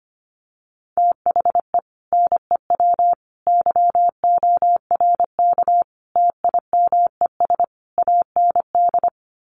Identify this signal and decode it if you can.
Morse code, heard as a keyed tone, about 25 wpm, 700 Hz. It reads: THE NEW YORK TIMES AND
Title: Sets of 5 Words Encore 25wpm